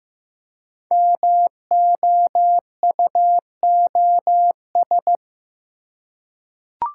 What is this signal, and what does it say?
Morse code, heard as a keyed tone, about 15 wpm, 700 Hz. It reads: MOUOS
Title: Code Groups - 5 Letters - 15wpm